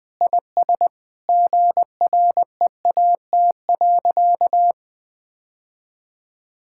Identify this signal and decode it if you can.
Morse code, heard as a keyed tone, about 21 wpm, 700 Hz. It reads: IS GREAT.
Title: Sentences with 2-Word Phrases - 20wpm